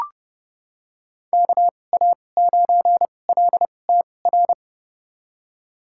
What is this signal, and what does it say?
Morse code, heard as a keyed tone, about 32 wpm, spaced as for 22 wpm, 700 Hz. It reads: KA9LTR
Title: USA Call Signs - 2x Character Spacing - 30wpm